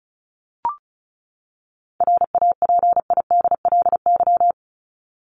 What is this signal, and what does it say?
Morse code, heard as a keyed tone, about 34 wpm, 700 Hz. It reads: RAPIDLY
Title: LY Suffix - Top 100 Words with Base Words 35wpm